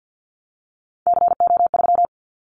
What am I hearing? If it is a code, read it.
Morse code, heard as a keyed tone, about 50 wpm, 700 Hz. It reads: CO3
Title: Callsign Prefixes for Top 100 DXCC - 50wpm